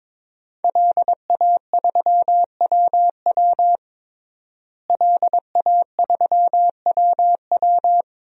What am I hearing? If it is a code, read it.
Morse code, heard as a keyed tone, about 22 wpm, 700 Hz. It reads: LA3WW LA3WW
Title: DXCC-100 Call Signs - 5 Characters - Repeated 2x - 22wpm